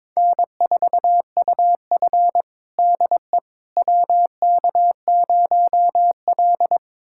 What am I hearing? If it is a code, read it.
Morse code, heard as a keyed tone, about 22 wpm, 700 Hz. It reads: N4UF DE WK0L
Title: USA Callsign DE Callsign - 4 Characters - 22wpm